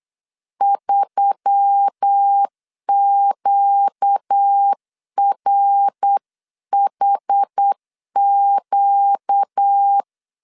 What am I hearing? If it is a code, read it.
Morse code, heard as a keyed tone, about 8 wpm, 800 Hz. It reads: 3QRHQ
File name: CW / DET / 1 6 1500C S B 8wpm 5142